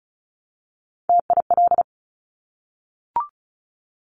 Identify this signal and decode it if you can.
Morse code, heard as a keyed tone, about 36 wpm, 700 Hz. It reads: TIL
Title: Top Three Letter Combinations in Words 35wpm